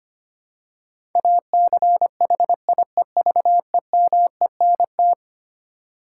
Morse code, heard as a keyed tone, about 25 wpm, 700 Hz. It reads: ACHIEVEMENT
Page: morse-code-ninja-practice-25wpm